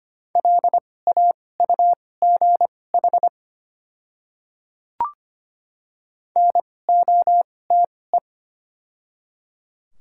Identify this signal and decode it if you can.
Morse code, heard as a keyed tone, about 25 wpm, spaced as for 18 wpm, 700 Hz. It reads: LAUGH NOTE
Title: 2x Character Spacing - Top 500 Words - 25wpm